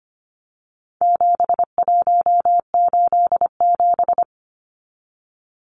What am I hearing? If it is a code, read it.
Morse code, heard as a keyed tone, about 25 wpm, 700 Hz. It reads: 7187
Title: Numbers - 4-Digit 25wpm